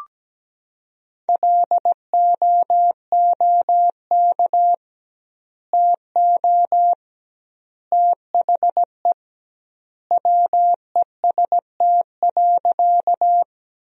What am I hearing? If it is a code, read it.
Morse code, heard as a keyed tone, about 17 wpm, 700 Hz. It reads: LOOK TO THE WEST.
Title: 2x Word-Spacing - Sentences from Top 400 Words - 17wpm